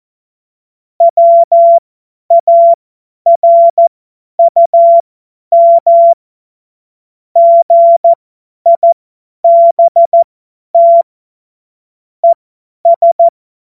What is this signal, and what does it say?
Morse code, heard as a keyed tone, about 14 wpm, spaced as for 10 wpm, 680 Hz. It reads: WARUM GIBT ES
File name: Morsekurs / Witze / Witz 03 Juni 2025 50 F70